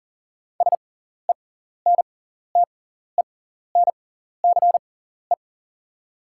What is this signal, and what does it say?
Morse code, heard as a keyed tone, about 40 wpm, spaced as for 14 wpm, 690 Hz. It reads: SENTENCE
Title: 6x Character Spacing - Top 300 Words - 40wpm